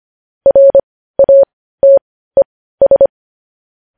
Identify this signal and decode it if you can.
Morse code, heard as a keyed tone, about 25 wpm, spaced as for 15 wpm, 550 Hz. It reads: RATES
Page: news-headlines-in-morse-code-at-15-wpm